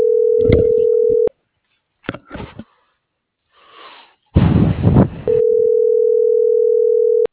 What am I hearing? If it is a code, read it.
Ringback tone (North America), 440 + 480 Hz.